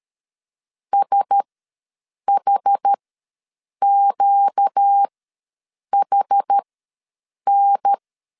Morse code, heard as a keyed tone, about 13 wpm, spaced as for 7 wpm, 800 Hz. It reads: SHQHN
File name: LEZIONI / 5 LBVC / 5 L ws 12wpm 2958